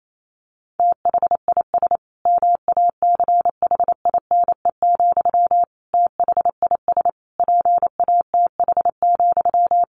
Morse code, heard as a keyed tone, about 28 wpm, 700 Hz. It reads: THIS MACHINE, THIS PATH,
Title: A Sound Of Thunder 28wpm